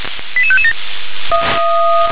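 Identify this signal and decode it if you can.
Five-tone selective call (ZVEI), heard as quick runs of five tones, tones of about 70 ms. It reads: 88557
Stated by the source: ZVEI five-tone call received by SDR.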